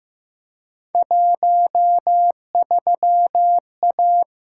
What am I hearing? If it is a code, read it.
Morse code, heard as a keyed tone, about 15 wpm, 700 Hz. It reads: 13A
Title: Field Day Exchanges Encore - 15wpm